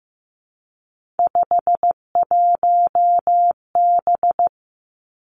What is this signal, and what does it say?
Morse code, heard as a keyed tone, about 15 wpm, 700 Hz. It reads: 51B
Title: Three Challenging Letter-Number Combinations 15wpm